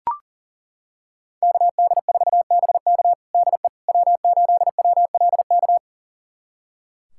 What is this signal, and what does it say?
Morse code, heard as a keyed tone, about 37 wpm, 700 Hz. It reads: KD4BK DE W8WLK
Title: USA Callsign DE Callsign 40wpm